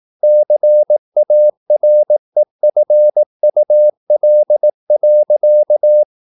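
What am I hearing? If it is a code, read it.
Morse code, heard as a keyed tone, about 18 wpm, 600 Hz. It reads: CAREFUL.
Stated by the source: Morse practice at 18 wpm, English text, from CW med Gnister.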